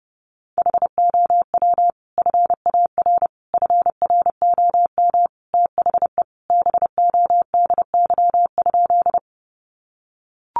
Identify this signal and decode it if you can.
Morse code, heard as a keyed tone, about 30 wpm, 700 Hz. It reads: HOW FAR FROM THE BODY?
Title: Sentences from Top 400 Words 30wpm